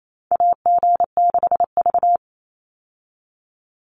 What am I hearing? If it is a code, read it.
Morse code, heard as a keyed tone, about 28 wpm, 700 Hz. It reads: AG6V
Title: Call Signs - 28wpm